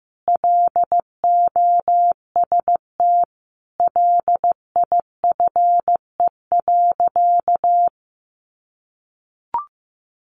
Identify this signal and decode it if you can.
Morse code, heard as a keyed tone, about 15 wpm, 700 Hz. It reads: LOST LIFE.
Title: Sentences from Top 1400 Words 15wpm